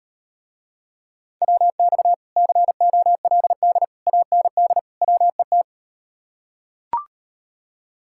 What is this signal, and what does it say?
Morse code, heard as a keyed tone, about 38 wpm, 700 Hz. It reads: WX COLD AND WET